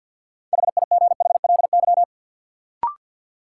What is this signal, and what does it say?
Morse code, heard as a keyed tone, about 52 wpm, 700 Hz. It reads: HIGHLY
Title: Sentences from Top 1100 Words - New Word Drill Alt - 50wpm